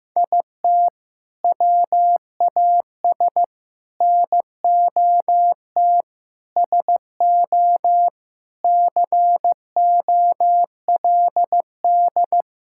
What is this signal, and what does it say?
Morse code, heard as a keyed tone, about 15 wpm, 700 Hz. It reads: IT WAS NOT SO COLD
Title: Sentences from Top 1600 Words 15wpm